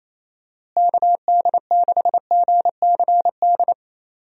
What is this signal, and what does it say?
Morse code, heard as a keyed tone, about 28 wpm, 700 Hz. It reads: KD6GCD
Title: USA Call Signs - 6 Characters - Repeated 2x - 28wpm